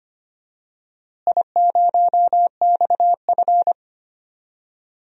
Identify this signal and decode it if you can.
Morse code, heard as a keyed tone, about 25 wpm, 700 Hz. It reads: I0XF
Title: DXCC-100 Call Signs - 4 Characters - Repeated 2x - 25wpm